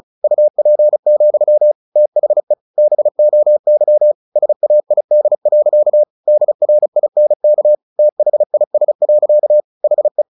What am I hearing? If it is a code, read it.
Morse code, heard as a keyed tone, about 36 wpm, 600 Hz. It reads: UP, THE BOY SAID. DRINK THIS. HE
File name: the old man and the sea 2 35wpm